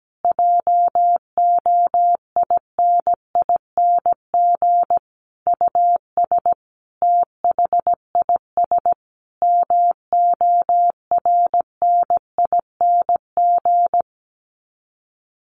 Morse code, heard as a keyed tone, about 17 wpm, 700 Hz. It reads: JOINING US THIS MORNING
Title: Sets of 5 Words Encore 17wpm